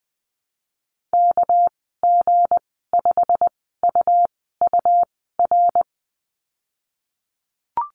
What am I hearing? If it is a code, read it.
Morse code, heard as a keyed tone, about 20 wpm, 700 Hz. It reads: K G 5 U U R
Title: 231 Lesson 37g - Letter x - Call Signs 20wpm